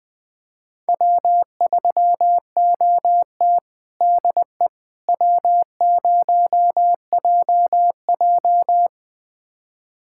Morse code, heard as a keyed tone, about 20 wpm, 700 Hz. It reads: W3OT DE W0JJ
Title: USA Callsign DE Callsign - 4 Characters - 20wpm